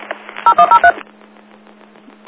DTMF, heard as telephone touch tones, tones of about 80 ms. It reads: 0103